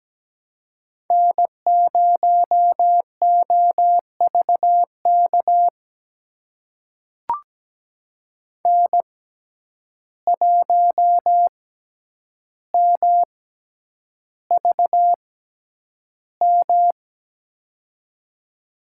Morse code, heard as a keyed tone, about 17 wpm, spaced as for 9 wpm, 700 Hz. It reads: N0OVK N1MVM
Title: USA Call Signs - 5 Characters - 6x Character Spacing - 17wpm